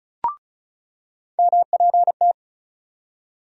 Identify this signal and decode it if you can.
Morse code, heard as a keyed tone, about 34 wpm, 700 Hz. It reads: MPT
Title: USA Call Sign Postfixes - 3 Letters 35wpm